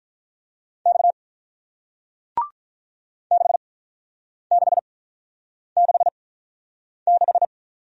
Morse code, heard as a keyed tone, about 48 wpm, 700 Hz. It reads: X 6 6 6 6